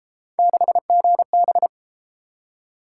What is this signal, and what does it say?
Morse code, heard as a keyed tone, about 31 wpm, 700 Hz. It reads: 6GB